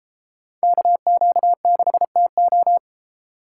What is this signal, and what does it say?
Morse code, heard as a keyed tone, about 30 wpm, 700 Hz. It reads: KQ6TO